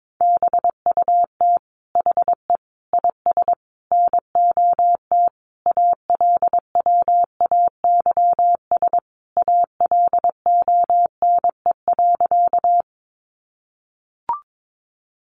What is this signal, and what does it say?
Morse code, heard as a keyed tone, about 22 wpm, 700 Hz. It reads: BUT HE IS NOT ALWAYS ALONE.
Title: Sentences from Top 1000 Words 22wpm